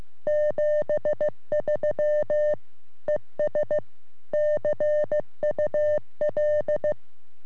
Morse code, heard as a keyed tone, about 15 wpm, 600 Hz. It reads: 73 ES CUL
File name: sm7lcb / ra3le 23cm